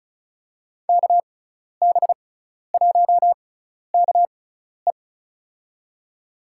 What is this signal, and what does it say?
Morse code, heard as a keyed tone, about 36 wpm, spaced as for 12 wpm, 700 Hz. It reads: KB1KE
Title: USA Call Signs - 5 Characters - 6x Character Spacing - 35wpm